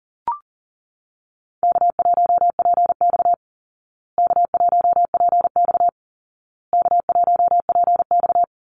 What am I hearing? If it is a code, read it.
Morse code, heard as a keyed tone, about 40 wpm, 700 Hz. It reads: K1PX K1PX K1PX
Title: Easy Call Signs Repeated 3x - 40wpm